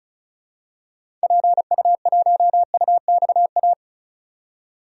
Morse code, heard as a keyed tone, about 36 wpm, 700 Hz. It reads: PU1UXA
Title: DXCC-100 Call Signs - 6 Characters - 35wpm